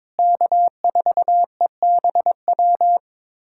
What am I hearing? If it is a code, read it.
Morse code, heard as a keyed tone, about 22 wpm, 700 Hz. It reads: K4EBW
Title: USA Callsign DE Callsign - 5 Characters - 22wpm